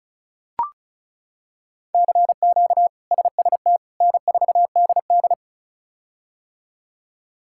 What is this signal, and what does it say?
Morse code, heard as a keyed tone, about 36 wpm, 700 Hz. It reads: CQ SST N4DD
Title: K1USN SST QSO Practice - 35wpm